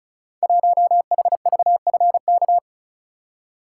Morse code, heard as a keyed tone, about 36 wpm, 700 Hz. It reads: 1HVFK
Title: Code Groups - 5 Letter:Numbers - 35wpm